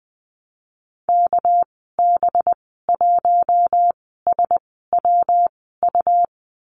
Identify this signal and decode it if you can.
Morse code, heard as a keyed tone, about 20 wpm, spaced as for 14 wpm, 700 Hz. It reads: KB1SWU
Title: 207 Lesson 33g - Letter j - Call Signs 20wpm